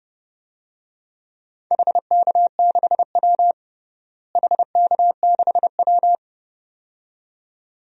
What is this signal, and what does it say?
Morse code, heard as a keyed tone, about 30 wpm, 700 Hz. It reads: HK6W HK6W
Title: DXCC-100 Call Signs - 4 Characters - Repeated 2x - 30wpm